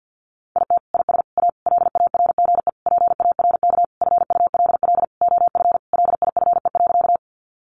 Morse code, heard as a keyed tone, about 50 wpm, 700 Hz. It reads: IT IS A LARGE PARK FULL OF LIFE.
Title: Sentences from Top 1300 Words 50wpm